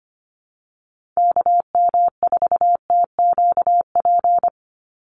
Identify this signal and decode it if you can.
Morse code, heard as a keyed tone, about 25 wpm, 700 Hz. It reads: KM4TQP
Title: USA Call Signs 25wpm